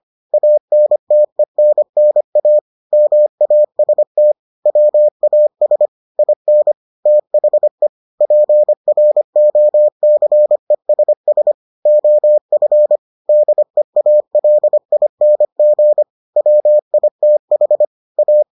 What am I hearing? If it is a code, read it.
Morse code, heard as a keyed tone, about 25 wpm, 600 Hz. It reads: ANTENNA MAST WAS IN THE PROCESS OF DEALING WITH A